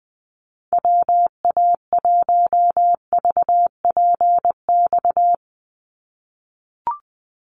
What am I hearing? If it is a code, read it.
Morse code, heard as a keyed tone, about 20 wpm, 700 Hz. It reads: WA1VPX
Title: USA Call Signs 20wpm